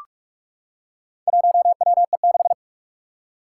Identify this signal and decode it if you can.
Morse code, heard as a keyed tone, about 41 wpm, 700 Hz. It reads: 1WE6
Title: Four Challenging Letter-Number Combinations 45wpm